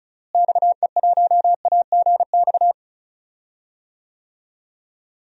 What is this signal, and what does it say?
Morse code, heard as a keyed tone, about 36 wpm, 700 Hz. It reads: XE1AGX
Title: DXCC-100 Call Signs - 6 Characters - Repeated 2x - 35wpm